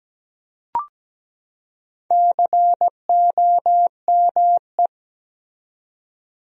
Morse code, heard as a keyed tone, about 17 wpm, 700 Hz. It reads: COME
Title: 4x Word-Spacing - Sentences from Top 200 Words - 17wpm